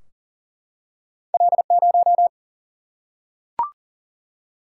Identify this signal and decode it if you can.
Morse code, heard as a keyed tone, about 42 wpm, 700 Hz. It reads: L0
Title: Single-Letter Single-Number 40wpm